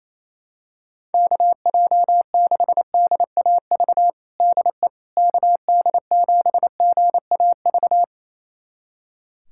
Morse code, heard as a keyed tone, about 28 wpm, 700 Hz. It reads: KJ6DAV DE KD7GAV
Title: USA Callsign DE Callsign - 6 Characters - 28wpm